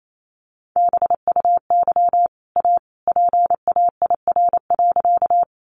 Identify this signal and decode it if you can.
Morse code, heard as a keyed tone, about 28 wpm, 700 Hz. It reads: BUY A PAIR.
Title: Sentences from Top 1200 Words 28wpm